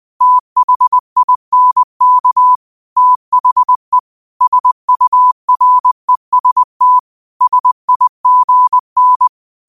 Morse code, heard as a keyed tone, about 20 wpm, 1000 Hz. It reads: THINK THE SUREST SIGN